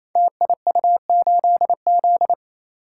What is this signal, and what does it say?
Morse code, heard as a keyed tone, about 27 wpm, 700 Hz. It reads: TIU8Z
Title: Code Groups - 5 Letter:Numbers - 28wpm